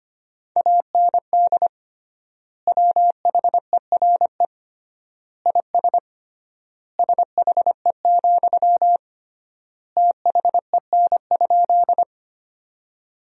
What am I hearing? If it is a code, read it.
Morse code, heard as a keyed tone, about 25 wpm, 700 Hz. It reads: AND WHERE IS SHE, THEN?
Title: 3x Word-Spacing - Sentences from Top 200 Words - 25wpm